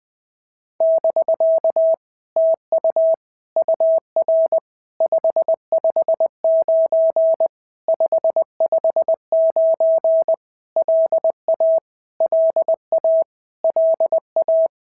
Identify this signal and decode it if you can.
Morse code, heard as a keyed tone, about 20 wpm, 650 Hz. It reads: <BK> TU UR 559 559 LA LA LA
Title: POTA QSO Practice - 20wpm